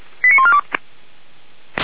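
Five-tone selective call (ZVEI), heard as quick runs of five tones, tones of about 70 ms. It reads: 89142
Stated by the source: ZVEI five-tone selective call received with an SDR, archived by Feuerwehr Orlamünde.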